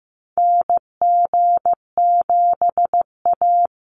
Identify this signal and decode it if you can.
Morse code, heard as a keyed tone, about 15 wpm, 700 Hz. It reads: NG7A
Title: Call Signs Repeated 3x - 15wpm